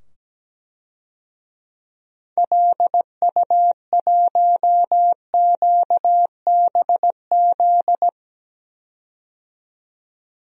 Morse code, heard as a keyed tone, about 17 wpm, 700 Hz. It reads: LU1QBZ
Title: DXCC-100 Call Signs - 6 Characters - 17wpm